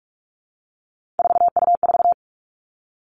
Morse code, heard as a keyed tone, about 46 wpm, 700 Hz. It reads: 4U4
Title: Three Challenging Letter-Number Combinations 45wpm